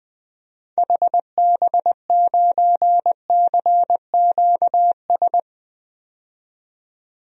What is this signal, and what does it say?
Morse code, heard as a keyed tone, about 21 wpm, 700 Hz. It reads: HB9CQS
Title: DXCC-100 Call Signs - 6 Characters - 20wpm